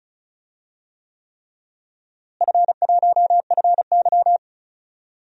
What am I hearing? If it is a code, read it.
Morse code, heard as a keyed tone, about 34 wpm, 700 Hz. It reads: F1FY
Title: DXCC-100 Call Signs - 4 Characters - Repeated 3x - 35wpm